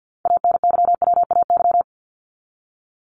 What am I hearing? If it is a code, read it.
Morse code, heard as a keyed tone, about 50 wpm, 700 Hz. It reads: ANYWAY
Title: Top 1501-1750 Words 50wpm